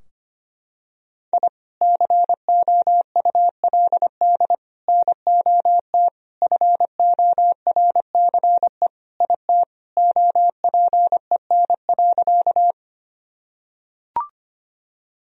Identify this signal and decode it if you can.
Morse code, heard as a keyed tone, about 25 wpm, 700 Hz. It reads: I COULD NOT FORCE IT OPEN.
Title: Sentences from Top 1000 Words - Review Part 3 - 25wpm